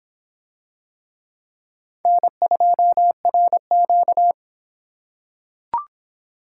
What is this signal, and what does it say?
Morse code, heard as a keyed tone, about 25 wpm, 700 Hz. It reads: N2RQ